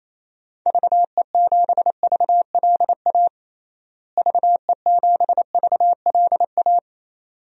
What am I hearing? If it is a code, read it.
Morse code, heard as a keyed tone, about 28 wpm, 700 Hz. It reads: VE7VLA VE7VLA
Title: DXCC-100 Call Signs - 6 Characters - Repeated 2x - 28wpm